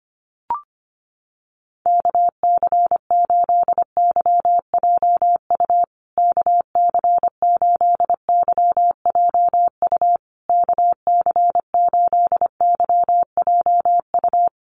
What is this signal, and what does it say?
Morse code, heard as a keyed tone, about 25 wpm, 700 Hz. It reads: KC8YJU KC8YJU KC8YJU
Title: USA Call Signs Repeated 3x 25wpm